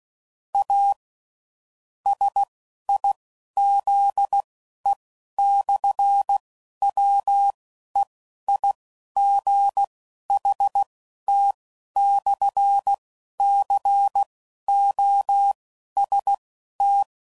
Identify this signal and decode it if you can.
Morse code, heard as a keyed tone, about 16 wpm, spaced as for 11 wpm, 800 Hz. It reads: A SIZE/WEIGHT/COST
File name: CW Code Groups @10WPM / Technical Subjects / SwPwrSup